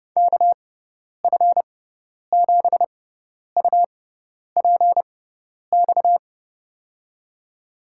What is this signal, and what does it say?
Morse code, heard as a keyed tone, about 30 wpm, spaced as for 10 wpm, 700 Hz. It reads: KF7UPX
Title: USA Call Signs - 6x Character Spacing - 30wpm